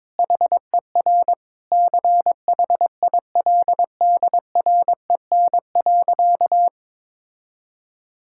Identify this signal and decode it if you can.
Morse code, heard as a keyed tone, about 21 wpm, 700 Hz. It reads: HER CHILDREN.
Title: Sentences from Top 1000 Words - Review Part 5 - 22wpm